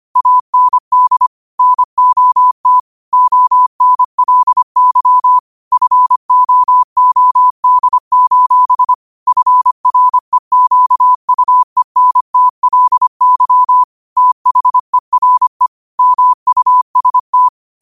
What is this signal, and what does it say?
Morse code, heard as a keyed tone, about 25 wpm, 1000 Hz. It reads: AND NOT ONLY FOOD: FREQUENTLY THERE MUST
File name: 25wpm